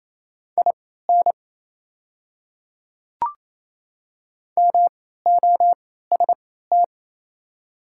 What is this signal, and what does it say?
Morse code, heard as a keyed tone, about 28 wpm, spaced as for 16 wpm, 690 Hz. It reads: IN MOST
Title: 3x Character Spacing - Top 100 Words - 28wpm